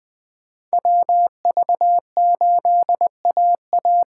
Morse code, heard as a keyed tone, about 21 wpm, 700 Hz. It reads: WV8AA
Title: Speed Racing - USA 2x2 Call Signs 17wpm